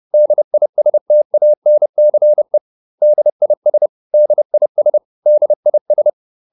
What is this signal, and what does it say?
Morse code, heard as a keyed tone, about 30 wpm, 600 Hz. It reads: DISTANCE DIS DIS DIS